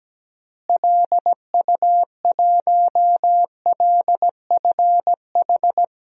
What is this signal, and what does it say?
Morse code, heard as a keyed tone, about 17 wpm, 700 Hz. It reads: LU1LFH
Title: DXCC-100 Call Signs - 6 Characters - Repeated 3x - 17wpm